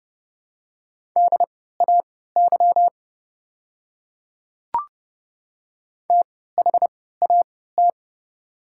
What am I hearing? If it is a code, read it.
Morse code, heard as a keyed tone, about 30 wpm, spaced as for 17 wpm, 700 Hz. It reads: DAY THAT
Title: 3x Character Spacing - Top 100 Words - 30wpm